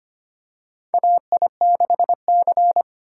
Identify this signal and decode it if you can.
Morse code, heard as a keyed tone, about 25 wpm, 700 Hz. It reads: AI6C
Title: USA Call Signs - 4x Character Spacing - 25wpm